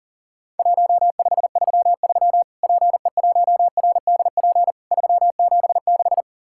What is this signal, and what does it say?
Morse code, heard as a keyed tone, about 41 wpm, 700 Hz. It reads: 1533 PE1RDP 376
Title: Worked All Europe DX Contest - QTC Exchange - 40wpm